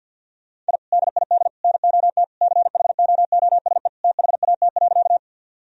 Morse code, heard as a keyed tone, about 50 wpm, 700 Hz. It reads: I DID NOT CHOOSE THAT.
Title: Sentences from Top 1000 Words - Review Part 4 - 50wpm